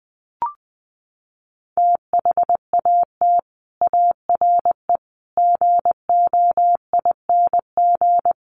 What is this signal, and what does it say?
Morse code, heard as a keyed tone, about 20 wpm, 700 Hz. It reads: THAT ARE GOING
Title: Sets of 5 Words Encore2 20wpm